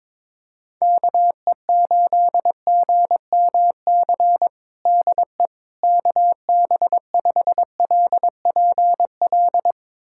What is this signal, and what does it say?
Morse code, heard as a keyed tone, about 22 wpm, 700 Hz. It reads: KE8GMC DE KB5LPL
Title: USA Callsign DE Callsign - 6 Characters - 22wpm